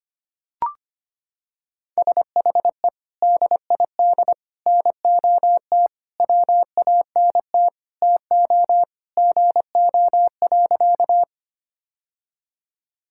Morse code, heard as a keyed tone, about 25 wpm, 700 Hz. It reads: SHE DID NOT WANT TO GO.
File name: practice / 25wpm / Sentences with 4-Word Phrases - 25wpm